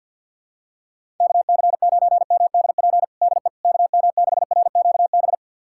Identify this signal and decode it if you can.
Morse code, heard as a keyed tone, about 47 wpm, 690 Hz. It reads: KC9MDP DE KM6RQB
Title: USA Callsign DE Callsign - 6 Characters - 50wpm